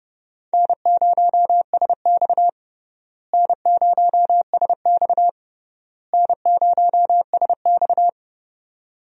Morse code, heard as a keyed tone, about 30 wpm, 700 Hz. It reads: N0SX N0SX N0SX
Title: USA Call Signs - 4 Characters - Repeated 3x - 30wpm